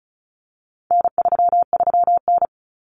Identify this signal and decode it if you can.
Morse code, heard as a keyed tone, about 34 wpm, 700 Hz. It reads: N33N
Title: Common Cut Numbers - 4-Digit 35wpm